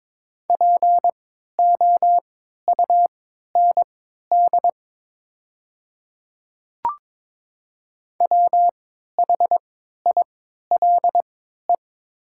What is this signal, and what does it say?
Morse code, heard as a keyed tone, about 22 wpm, spaced as for 12 wpm, 700 Hz. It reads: POUND WHILE
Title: 3x Character Spacing - Top 500 Words - 22wpm